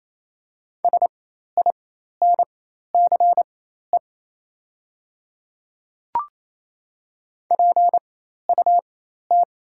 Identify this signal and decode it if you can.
Morse code, heard as a keyed tone, about 28 wpm, spaced as for 13 wpm, 690 Hz. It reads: SINCE PUT
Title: 4x Character Spacing - Top 500 Words - 28wpm